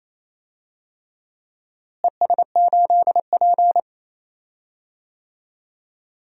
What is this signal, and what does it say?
Morse code, heard as a keyed tone, about 28 wpm, 700 Hz. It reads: ES8P
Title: DXCC-100 Call Signs - 4 Characters - Repeated 3x - 28wpm